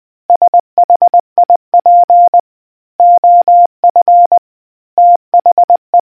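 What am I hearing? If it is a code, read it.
Morse code, heard as a keyed tone, about 20 wpm, 700 Hz. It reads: SHIP OF THE
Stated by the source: Morse quiz question from a New Zealand navy communicators association.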